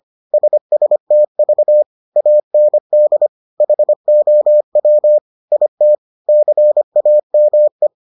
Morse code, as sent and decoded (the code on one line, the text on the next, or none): ... ... - ...- / .- -. -.. / .... --- .-- / .. - / -.-. .- -- .
SSTV AND HOW IT CAME